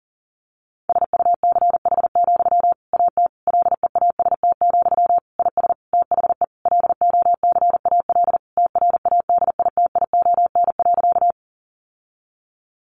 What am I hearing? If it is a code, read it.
Morse code, heard as a keyed tone, about 40 wpm, 700 Hz. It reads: SUCH, AT LEAST, IS THE LOCAL TRADITION.
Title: Sentences from Top 1500 Words 40wpm